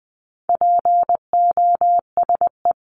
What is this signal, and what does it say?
Morse code, heard as a keyed tone, about 20 wpm, 700 Hz. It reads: POSE
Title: 260 Lesson 42e - Words 20-20wpm